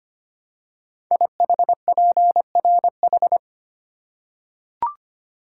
Morse code, heard as a keyed tone, about 25 wpm, 700 Hz. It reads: IHPRH
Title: Code Groups - 5 Letters - 25wpm